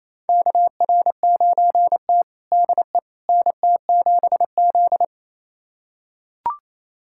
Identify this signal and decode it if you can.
Morse code, heard as a keyed tone, about 27 wpm, 700 Hz. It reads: KR9T DE NT7Z
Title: USA Callsign DE Callsign - 4 Characters - 28wpm